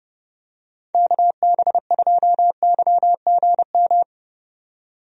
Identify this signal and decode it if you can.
Morse code, heard as a keyed tone, about 30 wpm, 700 Hz. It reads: KB2YGM
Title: USA Call Signs - 6 Characters - 30wpm